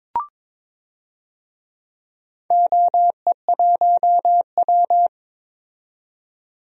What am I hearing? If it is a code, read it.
Morse code, heard as a keyed tone, about 22 wpm, 700 Hz. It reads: OE1W
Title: DXCC-100 Call Signs - 4 Characters - 22wpm